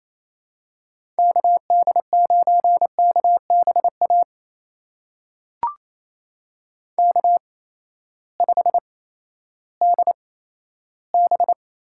Morse code, heard as a keyed tone, about 28 wpm, spaced as for 14 wpm, 700 Hz. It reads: KD9KBA K5DB
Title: USA Call Signs - 8x Character Spacing - 28wpm